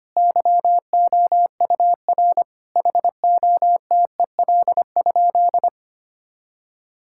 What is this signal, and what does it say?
Morse code, heard as a keyed tone, about 25 wpm, 700 Hz. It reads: YOUR HOTEL?
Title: Sentences from Top 1600 Words 25wpm